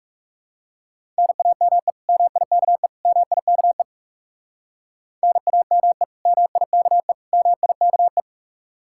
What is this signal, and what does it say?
Morse code, heard as a keyed tone, about 37 wpm, 700 Hz. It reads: NAME MIKE MIKE NAME MIKE MIKE
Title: Speed-Racing Common QSO Phrases Encore - 30wpm